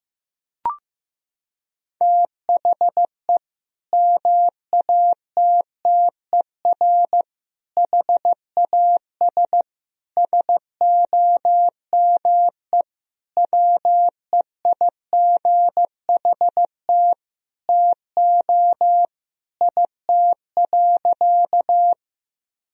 Morse code, heard as a keyed tone, about 15 wpm, 700 Hz. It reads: THE MATTER HAS SOME WEIGHT TO IT.
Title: Sentences from Top 1000 Words - Review Part 3 - 15wpm